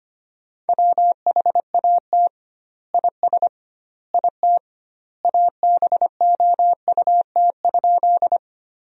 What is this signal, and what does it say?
Morse code, heard as a keyed tone, about 25 wpm, 700 Hz. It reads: WHAT IS IT ABOUT?
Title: 2x Word-Spacing - Sentences from Top 100 Words - 25wpm